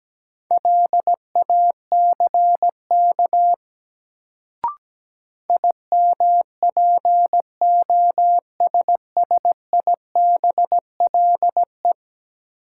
Morse code, heard as a keyed tone, about 17 wpm, 700 Hz. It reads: LACK IMPOSSIBLE
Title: Sentences from Top 1100 Words - New Word Drill Alt - 17wpm